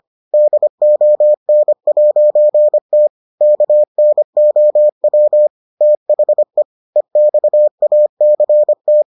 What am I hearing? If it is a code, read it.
Morse code, heard as a keyed tone, about 25 wpm, 600 Hz. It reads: DON'T KNOW THE EXACT